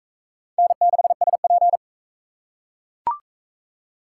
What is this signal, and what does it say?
Morse code, heard as a keyed tone, about 41 wpm, 700 Hz. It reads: N6SP